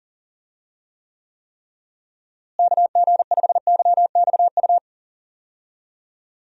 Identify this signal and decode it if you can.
Morse code, heard as a keyed tone, about 40 wpm, 700 Hz. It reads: KG5YXU
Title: USA Call Signs - 6 Characters - Repeated 3x - 40wpm